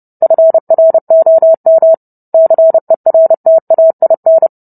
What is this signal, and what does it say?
Morse code, heard as a keyed tone, about 30 wpm, 650 Hz. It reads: FROM CERTAIN